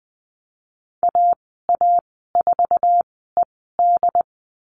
Morse code, heard as a keyed tone, about 20 wpm, spaced as for 15 wpm, 700 Hz. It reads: AA4ED
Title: 076 Lesson 13g - Letter L - Call Signs 20wpm